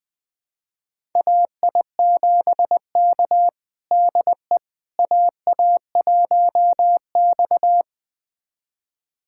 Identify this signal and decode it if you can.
Morse code, heard as a keyed tone, about 20 wpm, 700 Hz. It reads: AI7K DE AA1X
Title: USA Callsign DE Callsign - 4 Characters - 20wpm